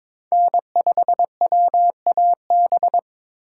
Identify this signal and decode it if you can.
Morse code, heard as a keyed tone, about 22 wpm, 700 Hz. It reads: N5WAB
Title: USA Call Signs - 5 Characters - 2x Character Spacing - 22wpm